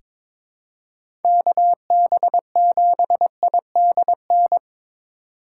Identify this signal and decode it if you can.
Morse code, heard as a keyed tone, about 22 wpm, 700 Hz. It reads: KB7IDN
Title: USA Call Signs - 6 Characters - 3x Character Spacing - 22wpm